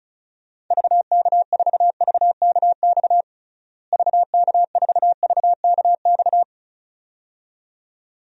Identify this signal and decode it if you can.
Morse code, heard as a keyed tone, about 34 wpm, 700 Hz. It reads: VK4VKX VK4VKX
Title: DXCC-100 Call Signs - 6 Characters - Repeated 2x - 35wpm